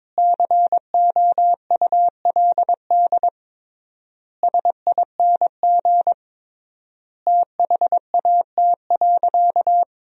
Morse code, heard as a keyed tone, about 22 wpm, 700 Hz. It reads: COULD SING THAT.
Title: 3x Word-Spacing - Sentences from Top 500 Words - 22wpm